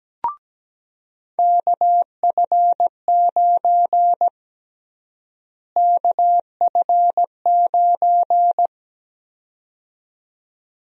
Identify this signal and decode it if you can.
Morse code, heard as a keyed tone, about 17 wpm, 700 Hz. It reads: KF9 KF9
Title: USA Call Sign Prefixes - Repeated 2x - 17wpm